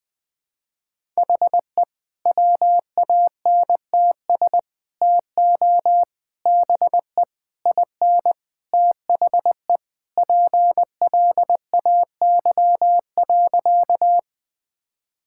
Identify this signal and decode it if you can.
Morse code, heard as a keyed tone, about 21 wpm, 700 Hz. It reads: HE WANTS TO BE IN THE PLAY.
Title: Sentences with 4-Word Phrases - 20wpm